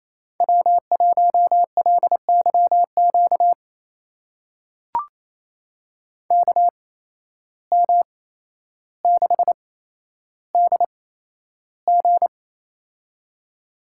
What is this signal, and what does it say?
Morse code, heard as a keyed tone, about 28 wpm, spaced as for 12 wpm, 700 Hz. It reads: W1LYQ KM6DG
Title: USA Call Signs - 5 Characters - 8x Character Spacing - 28wpm